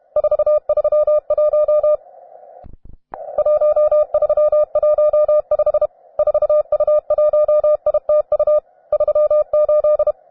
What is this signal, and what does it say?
Morse code, heard as a keyed tone, about 32 wpm, 600 Hz. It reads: 431 1315 4U1ITU 38
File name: cw advanced